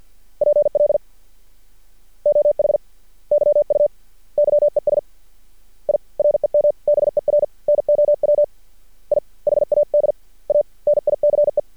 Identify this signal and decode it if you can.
Morse code, heard as a keyed tone, about 48 wpm, 590 Hz. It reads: PL OH YU YES I REMBER NOW I HAD A NICE
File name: archived / CRSnet